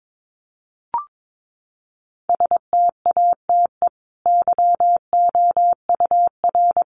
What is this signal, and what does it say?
Morse code, heard as a keyed tone, about 22 wpm, 700 Hz. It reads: STATE YOUR
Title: Sentences from Top 700 Words 22wpm